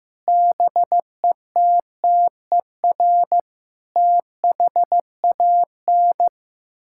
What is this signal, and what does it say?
Morse code, heard as a keyed tone, about 15 wpm, 700 Hz. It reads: BETTER THAN
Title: Sentences from Top 1000 Words - Review Part 6 - 15wpm